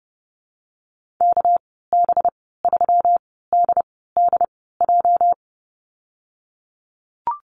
Morse code, heard as a keyed tone, about 30 wpm, spaced as for 17 wpm, 700 Hz. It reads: KB3DDJ
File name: Lesson 37g - Letter x - Call Signs 30wpm